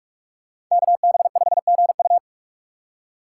Morse code, heard as a keyed tone, about 41 wpm, 700 Hz. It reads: KB5GU
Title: USA Call Signs - 5 Characters - 4x Character Spacing - 45wpm